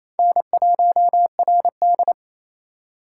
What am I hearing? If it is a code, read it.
Morse code, heard as a keyed tone, about 27 wpm, 700 Hz. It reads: N1RD
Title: USA Call Signs - 4 Characters - 4x Character Spacing - 28wpm